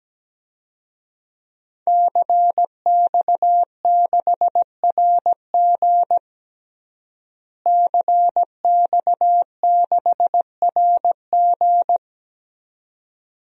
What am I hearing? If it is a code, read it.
Morse code, heard as a keyed tone, about 17 wpm, 700 Hz. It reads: CX6RG CX6RG
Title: DXCC-100 Call Signs - 5 Characters - Repeated 2x - 17wpm